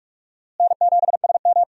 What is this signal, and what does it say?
Morse code, heard as a keyed tone, about 42 wpm, 700 Hz. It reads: N7SM
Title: USA Call Signs - 4 Characters - Repeated 2x - 45wpm